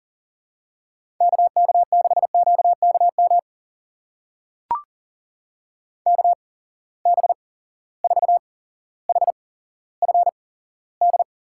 Morse code, heard as a keyed tone, about 40 wpm, spaced as for 17 wpm, 700 Hz. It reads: KK6QKM KB4HFD
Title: USA Call Signs - 6 Characters - 8x Character Spacing - 40wpm